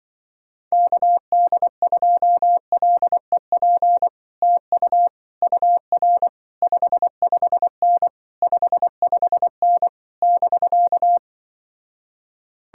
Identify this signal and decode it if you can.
Morse code, heard as a keyed tone, about 23 wpm, 700 Hz. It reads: KD2LEP TU UR 55N 55N <BK>
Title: Speed-Racing Common QSO Phrases - 20wpm